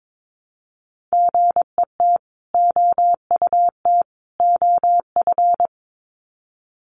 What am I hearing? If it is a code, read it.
Morse code, heard as a keyed tone, about 22 wpm, 700 Hz. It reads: GET OUT OF
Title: Sets of 3 Words 22wpm